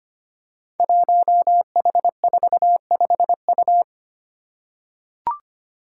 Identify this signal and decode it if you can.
Morse code, heard as a keyed tone, about 25 wpm, 700 Hz. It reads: 1H45U
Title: Five Challenging Letter-Number Combinations 25wpm